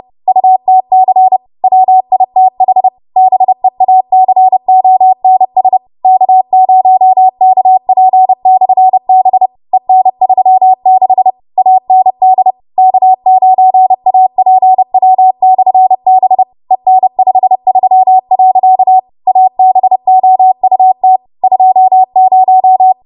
Morse code, heard as a keyed tone, about 30 wpm, 750 Hz. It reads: UTC WITH BEACONS K0KP/B EN36 AND K9APW/B EN53. ABOUT 20
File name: Morse / 30WPM